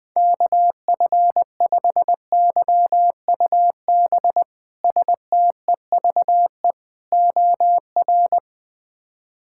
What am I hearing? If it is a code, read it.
Morse code, heard as a keyed tone, about 20 wpm, 700 Hz. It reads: KF5YUB STEVE OR
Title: K1USN SST QSO Practice - 20wpm